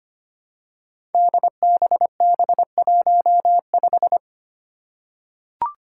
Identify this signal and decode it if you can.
Morse code, heard as a keyed tone, about 25 wpm, 700 Hz. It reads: DBB15
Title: Five Challenging Letter-Number Combinations 25wpm